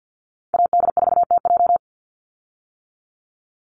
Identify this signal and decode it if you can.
Morse code, heard as a keyed tone, about 50 wpm, 700 Hz. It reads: AD4TJ
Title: Call Signs - 50wpm